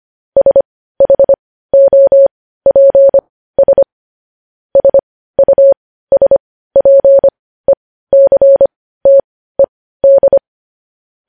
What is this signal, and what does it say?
Morse code, heard as a keyed tone, about 25 wpm, spaced as for 15 wpm, 550 Hz. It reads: SHOPS SUSPECTED
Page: news-headlines-in-morse-code-at-15-wpm